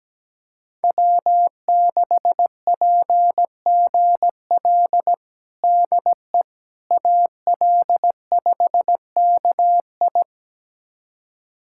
Morse code, heard as a keyed tone, about 17 wpm, 700 Hz. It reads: W6PGL DE AL5KI
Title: USA Callsign DE Callsign - 5 Characters - 17wpm